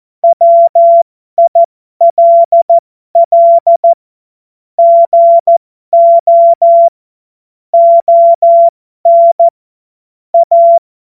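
Morse code, heard as a keyed tone, about 14 wpm, spaced as for 12 wpm, 680 Hz. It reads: WILL GO ON A
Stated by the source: Morse code joke.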